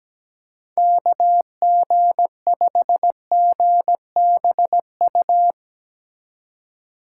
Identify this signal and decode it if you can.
Morse code, heard as a keyed tone, about 17 wpm, 700 Hz. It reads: KG5GBU
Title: USA Call Signs - 3x Character Spacing - 17wpm